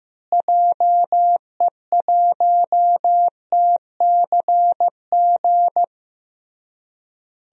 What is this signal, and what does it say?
Morse code, heard as a keyed tone, about 15 wpm, 700 Hz. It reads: JE1TCG
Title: DXCC-100 Call Signs - 6 Characters - Repeated 3x - 15wpm